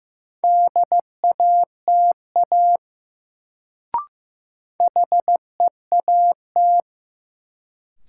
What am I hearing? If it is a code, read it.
Morse code, heard as a keyed tone, about 15 wpm, 700 Hz. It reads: DATA HEAT
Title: Sentences from Top 1000 Words - Review Part 3 - New Word Drill - 15wpm